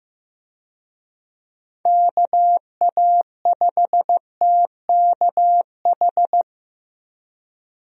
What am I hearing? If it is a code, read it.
Morse code, heard as a keyed tone, about 15 wpm, 700 Hz. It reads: KA5TKH
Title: USA Call Signs - 6 Characters - Repeated 2x - 15wpm